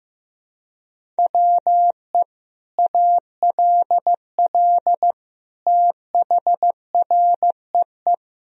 Morse code, heard as a keyed tone, about 15 wpm, 700 Hz. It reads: WE ALL THREE
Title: Sentences from Top 1000 Words - Review Part 1 - 15wpm